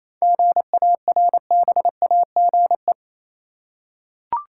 Morse code, heard as a keyed tone, about 27 wpm, 700 Hz. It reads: GARBAGE
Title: Seven Letter Words - 28wpm